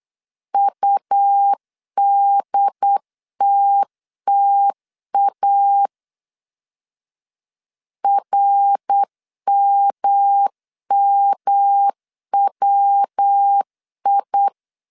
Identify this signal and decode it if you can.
Morse code, heard as a keyed tone, about 8 wpm, 800 Hz. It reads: UDTTA RMMWI
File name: CW / DET / 1 3 1500C S B 8wpm 4141